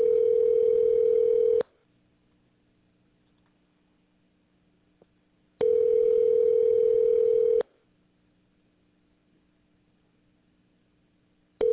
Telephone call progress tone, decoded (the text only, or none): ringback tone